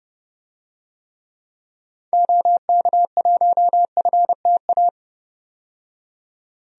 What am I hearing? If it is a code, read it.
Morse code, heard as a keyed tone, about 30 wpm, 700 Hz. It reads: OK1FTA
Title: DXCC-100 Call Signs - 6 Characters - 30wpm